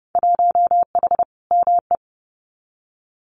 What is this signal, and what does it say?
Morse code, heard as a keyed tone, about 30 wpm, 700 Hz. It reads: 1H ME
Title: Winter Field Day Exchanges 30wpm